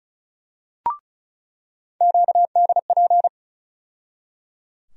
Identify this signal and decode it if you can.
Morse code, heard as a keyed tone, about 36 wpm, 700 Hz. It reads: QDP
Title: USA Call Sign Postfixes - 3 Letters 35wpm